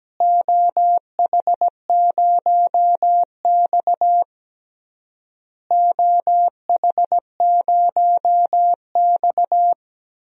Morse code, heard as a keyed tone, about 17 wpm, 700 Hz. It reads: OH0X OH0X
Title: DXCC-100 Call Signs - 4 Characters - Repeated 3x - 17wpm